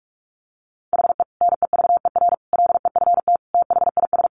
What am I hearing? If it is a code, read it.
Morse code, heard as a keyed tone, about 44 wpm, 700 Hz. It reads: HE NEVER LEFT THIS
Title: Sentences from Top 1100 Words 45wpm